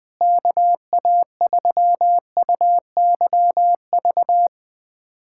Morse code, heard as a keyed tone, about 20 wpm, 700 Hz. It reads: KA3UYV
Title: USA Callsign DE Callsign - 6 Characters - 20wpm